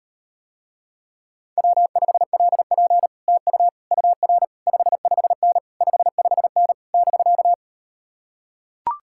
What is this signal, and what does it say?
Morse code, heard as a keyed tone, about 39 wpm, 700 Hz. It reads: W5LP TU UR 55N 55N <BK>